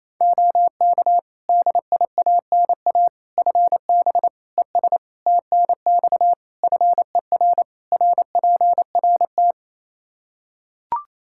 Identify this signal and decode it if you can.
Morse code, heard as a keyed tone, about 28 wpm, 700 Hz. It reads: OK DIANA FB ES TNX FER RPRT
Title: Practice QSO - Exchange 2 - Info - 28wpm